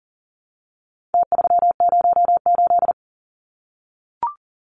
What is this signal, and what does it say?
Morse code, heard as a keyed tone, about 40 wpm, 700 Hz. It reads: T308
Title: Numbers - Contest Style - 4-Digit 40wpm